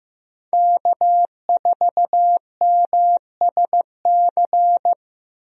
Morse code, heard as a keyed tone, about 15 wpm, 700 Hz. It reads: K4MSC
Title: USA Call Signs - 5 Characters - 15wpm